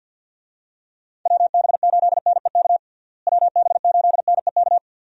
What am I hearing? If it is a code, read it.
Morse code, heard as a keyed tone, about 51 wpm, 700 Hz. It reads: WB8NEK WB8NEK
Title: USA Call Signs - 6 Characters - Repeated 2x - 50wpm